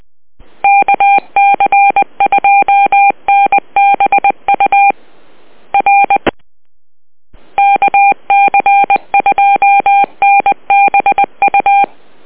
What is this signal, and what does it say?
Morse code, heard as a keyed tone, about 20 wpm, 800 Hz. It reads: KC2NBU R KC2NBU
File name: net audio